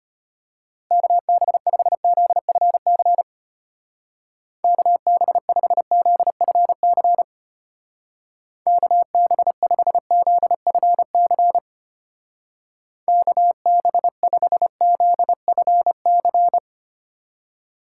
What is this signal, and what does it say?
Morse code, heard as a keyed tone, about 32 wpm, 700 Hz. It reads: KB5ZFC KB5ZFC KB5ZFC KB5ZFC